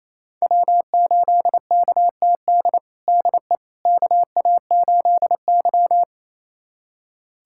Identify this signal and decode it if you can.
Morse code, heard as a keyed tone, about 28 wpm, 700 Hz. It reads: W8KTD DE KA8Y
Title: USA Callsign DE Callsign 28wpm